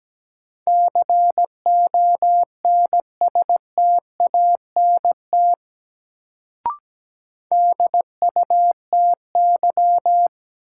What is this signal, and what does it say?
Morse code, heard as a keyed tone, about 17 wpm, 700 Hz. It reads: CONSTANT DUTY
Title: Sentences from Top 1500 Words - New Word Drill Alt - 17wpm